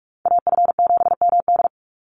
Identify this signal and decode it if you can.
Morse code, heard as a keyed tone, about 44 wpm, 700 Hz. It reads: AF7MD
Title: NA SOTA Chaser Callsigns 45wpm